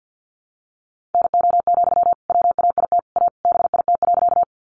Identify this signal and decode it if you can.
Morse code, heard as a keyed tone, about 50 wpm, 700 Hz. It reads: NO, WAIT A BIT.
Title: Sentences from Top 900 Words 50wpm